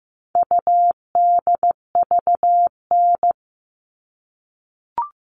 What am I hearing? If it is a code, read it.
Morse code, heard as a keyed tone, about 15 wpm, 700 Hz. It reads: UDVN
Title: All Cut Numbers - 4-Digit 15wpm